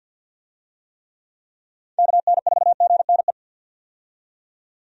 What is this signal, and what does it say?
Morse code, heard as a keyed tone, about 51 wpm, 700 Hz. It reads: KN4GNE
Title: USA Call Signs - 6 Characters - 50wpm